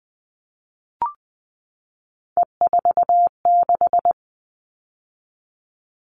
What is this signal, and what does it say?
Morse code, heard as a keyed tone, about 20 wpm, 700 Hz. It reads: E46
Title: All Cut Numbers - 3-Digit 20wpm